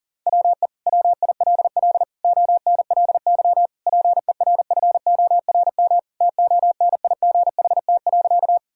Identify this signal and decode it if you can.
Morse code, heard as a keyed tone, about 40 wpm, 700 Hz. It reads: WE WILL ONLY PERFORM TONIGHT.
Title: Sentences from Top 1000 Words - Review Part 5 - 40wpm